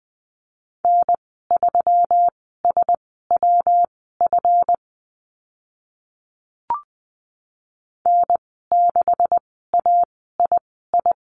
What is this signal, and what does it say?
Morse code, heard as a keyed tone, about 20 wpm, spaced as for 14 wpm, 700 Hz. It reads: N3SWF N6AII
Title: 140 Lesson 23g - Letter f - Call Signs 20wpm